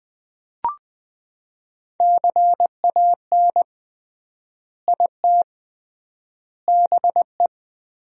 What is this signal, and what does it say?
Morse code, heard as a keyed tone, about 20 wpm, 700 Hz. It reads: CAN IT BE
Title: 3x Word-Spacing - Sentences from Top 400 Words - 20wpm